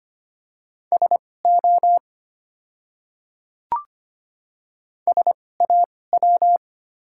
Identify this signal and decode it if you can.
Morse code, heard as a keyed tone, about 25 wpm, spaced as for 18 wpm, 700 Hz. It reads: SO SAW
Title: 2x Character Spacing - Top 300 Words - 25wpm